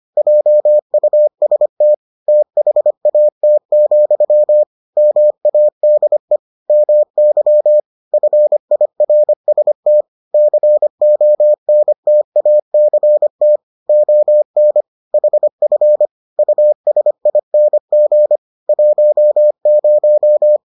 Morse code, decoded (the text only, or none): JUST THAT, MADE MY FIRST CONTACT ON HF USING 10